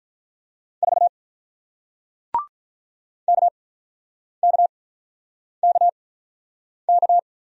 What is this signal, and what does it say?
Morse code, heard as a keyed tone, about 48 wpm, spaced as for 9 wpm, 700 Hz. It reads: 4 KKKK